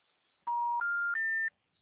Special information tone.